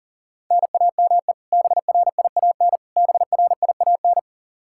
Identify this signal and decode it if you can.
Morse code, heard as a keyed tone, about 40 wpm, 700 Hz. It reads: NAME BRIAN BRIAN
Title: K1USN SST QSO Practice 40wpm